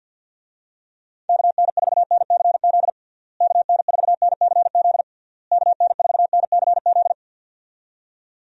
Morse code, heard as a keyed tone, about 50 wpm, 700 Hz. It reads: KN4NCZ KN4NCZ KN4NCZ
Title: USA Call Signs - 6 Characters - Repeated 3x - 50wpm